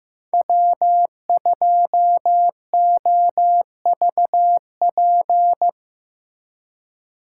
Morse code, heard as a keyed tone, about 18 wpm, 700 Hz. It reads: W2OVP
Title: Speed Racing - USA 1x3 Call Signs 15wpm